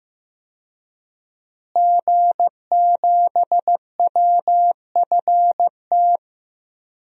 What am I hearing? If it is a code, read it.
Morse code, heard as a keyed tone, about 15 wpm, 700 Hz. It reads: G7WFT
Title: DXCC-100 Call Signs - 5 Characters - Repeated 2x - 15wpm